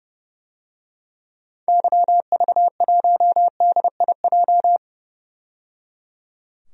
Morse code, heard as a keyed tone, about 30 wpm, 700 Hz. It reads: YV1DIJ
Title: DXCC-100 Call Signs - 6 Characters - 30wpm